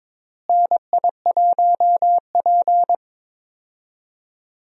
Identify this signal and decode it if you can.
Morse code, heard as a keyed tone, about 23 wpm, 700 Hz. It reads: NI1P
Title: USA Call Signs - 4 Characters - Repeated 3x - 22wpm